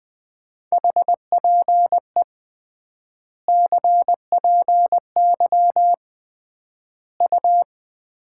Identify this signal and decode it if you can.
Morse code, heard as a keyed tone, about 21 wpm, 700 Hz. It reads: HPE CPY U
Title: Common QSO Phrases - 3x Word-Spacing - 20wpm